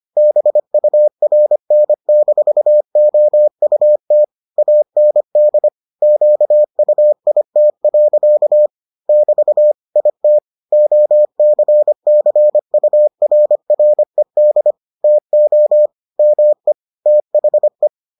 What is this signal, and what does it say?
Morse code, heard as a keyed tone, about 25 wpm, 600 Hz. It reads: BURN-OUT AND QUIT. <BT> IT OCCURRED TO ME THE